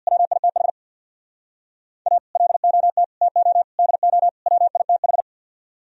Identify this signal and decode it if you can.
Morse code, heard as a keyed tone, about 64 wpm, 700 Hz. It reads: WITH A LOT TO DO WITH